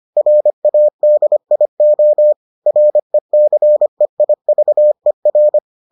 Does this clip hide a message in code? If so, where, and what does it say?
Yes: Morse code from 0.2 to 5.6 s, heard as a keyed tone, about 25 wpm, 600 Hz. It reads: RADIO RECEIVER